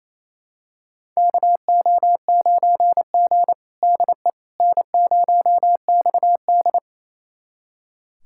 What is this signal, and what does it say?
Morse code, heard as a keyed tone, about 28 wpm, 700 Hz. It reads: KO9G DE N0XD
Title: USA Callsign DE Callsign - 4 Characters - 28wpm